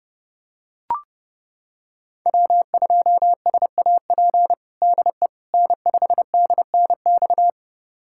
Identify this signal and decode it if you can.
Morse code, heard as a keyed tone, about 32 wpm, 700 Hz. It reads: W2SAP DE N5DNX
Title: USA Callsign DE Callsign - 5 Characters - 30wpm